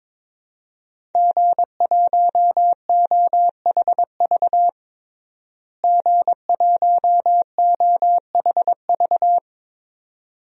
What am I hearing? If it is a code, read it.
Morse code, heard as a keyed tone, about 22 wpm, 700 Hz. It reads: G1OHV G1OHV
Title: DXCC-100 Call Signs - 5 Characters - Repeated 2x - 22wpm